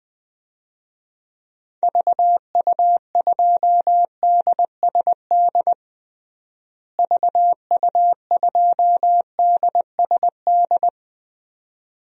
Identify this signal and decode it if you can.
Morse code, heard as a keyed tone, about 20 wpm, 700 Hz. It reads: VU2DSD VU2DSD
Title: DXCC-100 Call Signs - 6 Characters - Repeated 2x - 20wpm